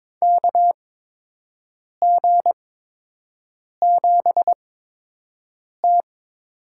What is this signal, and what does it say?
Morse code, heard as a keyed tone, about 23 wpm, spaced as for 6 wpm, 700 Hz. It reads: KG7T
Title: USA Call Signs - 8x Character Spacing - 22wpm